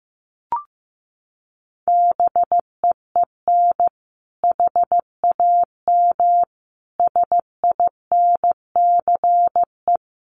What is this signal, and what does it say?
Morse code, heard as a keyed tone, about 14 wpm, 700 Hz. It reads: BEEN HAM SINCE
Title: Common QSO Phrases 15wpm